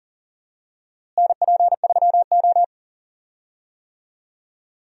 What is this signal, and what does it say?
Morse code, heard as a keyed tone, about 40 wpm, 700 Hz. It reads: NP3O
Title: USA Call Signs - 4 Characters - Repeated 2x - 40wpm